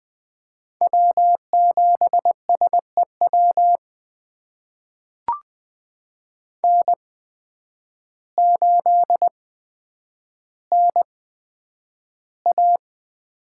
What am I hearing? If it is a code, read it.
Morse code, heard as a keyed tone, about 20 wpm, spaced as for 9 wpm, 700 Hz. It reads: W7SEW N8NA
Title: USA Call Signs - 5 Characters - 8x Character Spacing - 20wpm